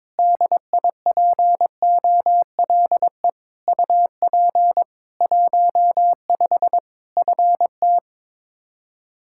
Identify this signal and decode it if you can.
Morse code, heard as a keyed tone, about 22 wpm, 700 Hz. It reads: DIPOLE UP 15 FT
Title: K1USN SST QSO Practice 22wpm